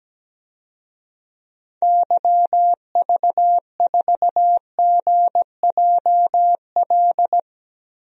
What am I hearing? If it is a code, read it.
Morse code, heard as a keyed tone, about 17 wpm, 700 Hz. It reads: YV4GJL
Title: DXCC-100 Call Signs - 6 Characters - Repeated 2x - 17wpm